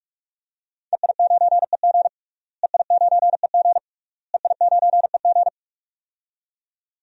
Morse code, heard as a keyed tone, about 48 wpm, 700 Hz. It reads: EI9EG EI9EG EI9EG
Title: DXCC-100 Call Signs - 5 Characters - Repeated 3x - 45wpm